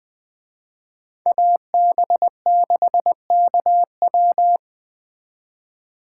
Morse code, heard as a keyed tone, about 21 wpm, 700 Hz. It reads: AB6KW
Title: Speed Racing - USA 2x2 Call Signs 17wpm